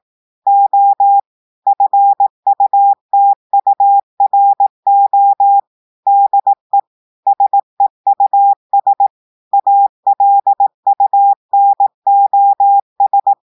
Morse code, heard as a keyed tone, about 18 wpm, 800 Hz. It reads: O FUTURO DE SEUS ALUNOS
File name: audios / portugais / portugais texte en clair 1 vitesse 18 mots minutes